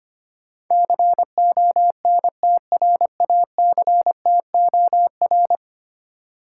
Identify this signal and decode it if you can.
Morse code, heard as a keyed tone, about 25 wpm, 700 Hz. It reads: CONTRACTOR
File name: Ten Letter Words - 25wpm